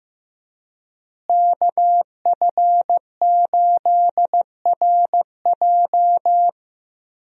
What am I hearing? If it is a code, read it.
Morse code, heard as a keyed tone, about 15 wpm, 700 Hz. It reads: KF8RJ
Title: USA Call Signs - 5 Characters - Repeated 3x - 15wpm